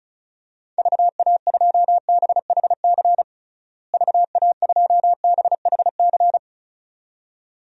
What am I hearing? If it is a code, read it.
Morse code, heard as a keyed tone, about 36 wpm, 700 Hz. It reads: VA2BHC VA2BHC
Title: DXCC-100 Call Signs - 6 Characters - Repeated 2x - 35wpm